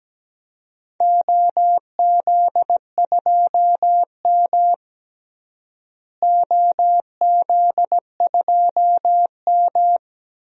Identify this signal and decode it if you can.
Morse code, heard as a keyed tone, about 17 wpm, 700 Hz. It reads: OZ2M OZ2M
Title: DXCC-100 Call Signs - 4 Characters - Repeated 2x - 17wpm